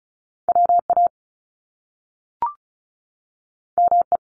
Morse code, heard as a keyed tone, about 36 wpm, 700 Hz. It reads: WA ME
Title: Top Two Letter Combinations in Words 35wpm